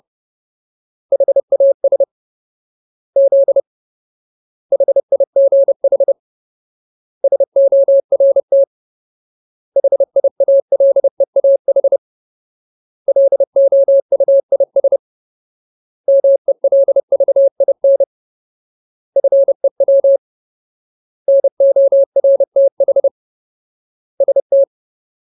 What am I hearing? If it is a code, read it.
Morse code, heard as a keyed tone, about 30 wpm, 550 Hz. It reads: HAS Z HIGH SORT HIALEAH LOUIS MELVIN FEW NORTH ST
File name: Ultimate Word List / 30WPM Ultimate